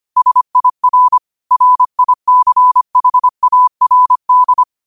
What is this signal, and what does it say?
Morse code, heard as a keyed tone, about 25 wpm, 1000 Hz. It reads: SIR RICHARD